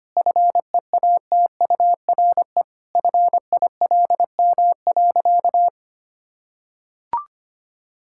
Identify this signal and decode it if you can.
Morse code, heard as a keyed tone, about 25 wpm, 700 Hz. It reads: FEATURE FILM.
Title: Sentences from Top 1000 Words - Review Part 5 - 25wpm